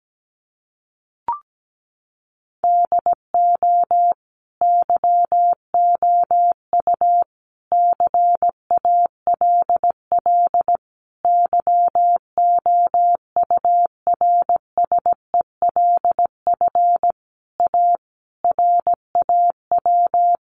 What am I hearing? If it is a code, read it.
Morse code, heard as a keyed tone, about 17 wpm, 700 Hz. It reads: DO YOU CALL YOURSELF A RAW
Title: Sentences from Top 1500 Words 17wpm